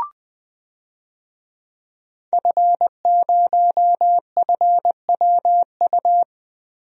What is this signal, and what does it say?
Morse code, heard as a keyed tone, about 20 wpm, 700 Hz. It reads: F0FWU
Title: DXCC-100 Call Signs - 5 Characters - Repeated 2x - 20wpm